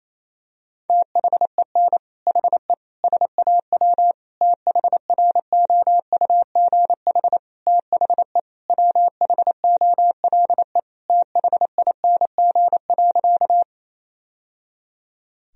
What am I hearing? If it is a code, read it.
Morse code, heard as a keyed tone, about 28 wpm, 700 Hz. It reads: THEN HE SAW THROUGH THE WHOLE THING.
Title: Sentences from Top 1000 Words - Review Part 1 - 28wpm